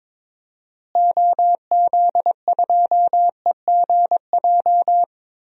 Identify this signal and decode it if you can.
Morse code, heard as a keyed tone, about 22 wpm, 700 Hz. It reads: OZ2EGJ
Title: DXCC-100 Call Signs - 6 Characters - Repeated 2x - 22wpm